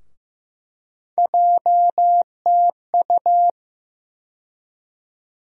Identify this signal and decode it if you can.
Morse code, heard as a keyed tone, about 15 wpm, 700 Hz. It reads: JTU
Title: USA Call Sign Postfixes - 3 Letters 15wpm